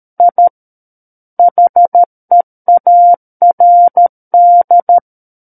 Morse code, heard as a keyed tone, about 13 wpm, 700 Hz. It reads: I HEARD
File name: Book One / 04-Chapter 4 - 13 WPM